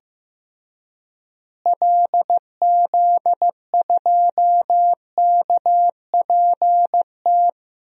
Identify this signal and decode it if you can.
Morse code, heard as a keyed tone, about 15 wpm, 700 Hz. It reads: LZ2KPT
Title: DXCC-100 Call Signs - 6 Characters - Repeated 3x - 15wpm